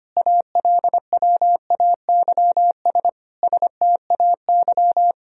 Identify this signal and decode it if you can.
Morse code, heard as a keyed tone, about 25 wpm, 700 Hz. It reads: ALWAYS STAY
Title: Sentences from Top 1000 Words - Review Part 1 - 25wpm